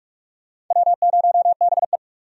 Morse code, heard as a keyed tone, about 46 wpm, 700 Hz. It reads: W0BE
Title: USA Call Signs - 4 Characters - 45wpm